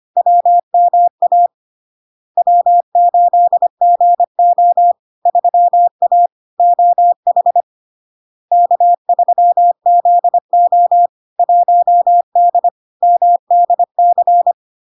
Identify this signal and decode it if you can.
Morse code, heard as a keyed tone, about 25 wpm, 700 Hz. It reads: WMA W8GO 3A OH K3ZO 1D MDC
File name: Log-in-CW-25wpm-700-Hz